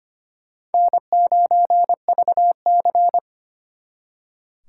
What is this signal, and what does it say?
Morse code, heard as a keyed tone, about 33 wpm, 700 Hz. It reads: N9VC